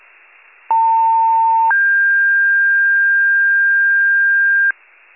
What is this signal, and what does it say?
Two-tone page, heard as a one-second tone, then a three-second tone, about 900 then 1600 Hz, with noise. It is heard from 0.7 to 4.7 s.